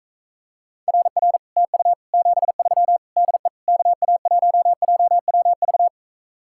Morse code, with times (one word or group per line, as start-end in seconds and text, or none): 0.9-1.4 RR
1.6-1.9 TU
2.1-3.0 73
3.2-3.5 DE
3.7-5.9 KA1JWV